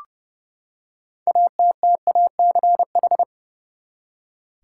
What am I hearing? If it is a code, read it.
Morse code, heard as a keyed tone, about 29 wpm, 700 Hz. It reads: ATTACH
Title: Six Letter Words - 30wpm